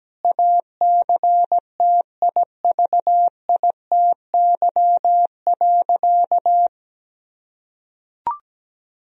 Morse code, heard as a keyed tone, about 17 wpm, 700 Hz. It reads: ACTIVITY.
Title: Sentences from Top 1000 Words - Review Part 4 - 17wpm